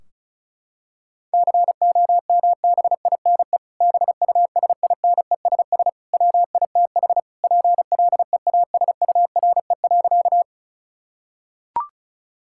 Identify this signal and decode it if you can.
Morse code, heard as a keyed tone, about 36 wpm, 700 Hz. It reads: COMBINE BUSINESS WITH PLEASURE.
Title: Sentences from Top 1600 Words 35wpm